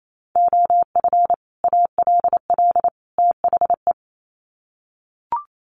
Morse code, heard as a keyed tone, about 28 wpm, 700 Hz. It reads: OF ALL THE
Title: Sets of 3 Words 28wpm